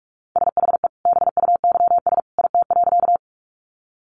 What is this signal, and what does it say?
Morse code, heard as a keyed tone, about 44 wpm, 700 Hz. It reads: SHE BUYS IT.
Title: Additional Easy Phrases 45wpm